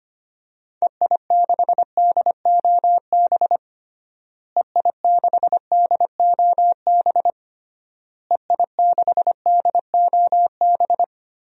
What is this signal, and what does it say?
Morse code, heard as a keyed tone, about 25 wpm, 700 Hz. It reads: EI6DOB EI6DOB EI6DOB
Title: DXCC-100 Call Signs - 6 Characters - Repeated 3x - 25wpm